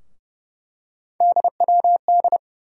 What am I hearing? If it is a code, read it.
Morse code, heard as a keyed tone, about 30 wpm, 700 Hz. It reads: DWD
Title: USA Call Sign Postfixes - 3 Letters 30wpm